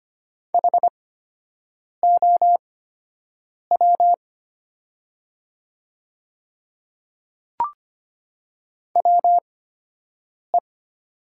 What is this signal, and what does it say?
Morse code, heard as a keyed tone, about 25 wpm, spaced as for 7 wpm, 700 Hz. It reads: HOW WE
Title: 8x Character Spacing - Top 200 Words - 25wpm